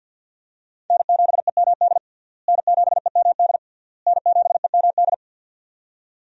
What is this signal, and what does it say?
Morse code, heard as a keyed tone, about 48 wpm, 700 Hz. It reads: N7EMD N7EMD N7EMD
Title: USA Call Signs - 5 Characters - Repeated 3x - 50wpm